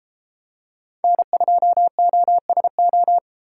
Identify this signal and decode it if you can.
Morse code, heard as a keyed tone, about 32 wpm, 700 Hz. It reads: N2OSO